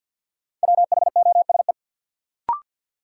Morse code, heard as a keyed tone, about 52 wpm, 700 Hz. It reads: WHOSE
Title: Sentences from Top 1200 Words - New Word Drill - 50wpm